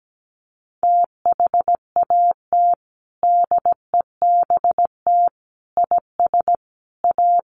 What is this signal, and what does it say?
Morse code, heard as a keyed tone, about 17 wpm, 700 Hz. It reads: THAT DEBT IS A
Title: Sentences from Top 1300 Words 17wpm